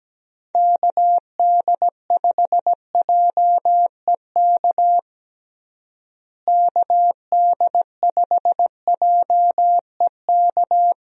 Morse code, heard as a keyed tone, about 17 wpm, 700 Hz. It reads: KD5JEK KD5JEK
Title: USA Call Signs - 6 Characters - Repeated 3x - 17wpm